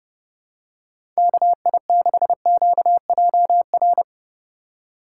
Morse code, heard as a keyed tone, about 30 wpm, 700 Hz. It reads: KI6QJR
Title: USA Call Signs - 6 Characters - 2x Character Spacing - 30wpm